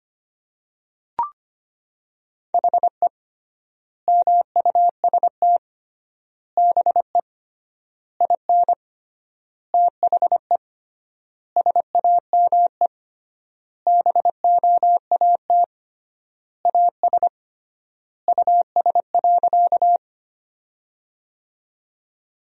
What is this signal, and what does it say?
Morse code, heard as a keyed tone, about 25 wpm, 700 Hz. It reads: HE MUST BE IN THE SAME BOAT AS US.
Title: 3x Word-Spacing - Sentences from Top 500 Words - 25wpm